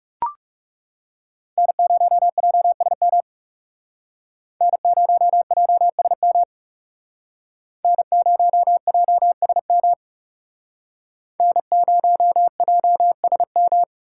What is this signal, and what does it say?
Morse code, heard as a keyed tone, about 38 wpm, 700 Hz. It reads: N0JSM N0JSM N0JSM N0JSM